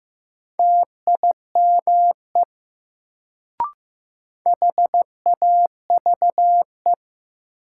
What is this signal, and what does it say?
Morse code, heard as a keyed tone, about 15 wpm, 700 Hz. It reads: TIME HAVE
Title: Sentences from Top 100 Words - New Word Drill - 15wpm